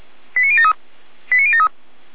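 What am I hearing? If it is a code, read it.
Five-tone selective call (ZVEI), heard as quick runs of five tones, tones of about 70 ms. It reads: 89073 89073
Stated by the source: ZVEI five-tone call 89119 received by SDR on channel 455.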